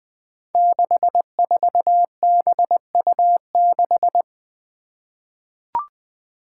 Morse code, heard as a keyed tone, about 20 wpm, 700 Hz. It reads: 64BU6
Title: Five Challenging Letter-Number Combinations 20wpm